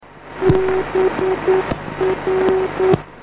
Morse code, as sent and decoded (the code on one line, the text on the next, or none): -... .-.
BR